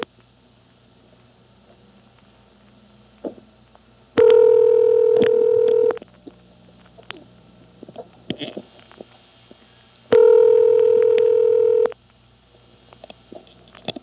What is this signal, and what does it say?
Ringback tone (North America), 440 + 480 Hz.
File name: MoreheadCityNCNoTest